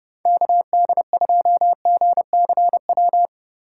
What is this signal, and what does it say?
Morse code, heard as a keyed tone, about 32 wpm, 700 Hz. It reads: KD2GCW
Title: USA Call Signs - 2x Character Spacing - 30wpm